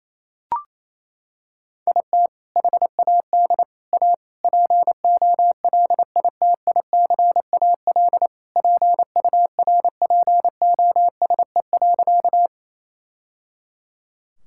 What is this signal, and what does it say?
Morse code, heard as a keyed tone, about 27 wpm, 700 Hz. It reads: IT HAD A POLITICAL PURPOSE.
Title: Sentences from Top 1000 Words - Review Part 5 - 28wpm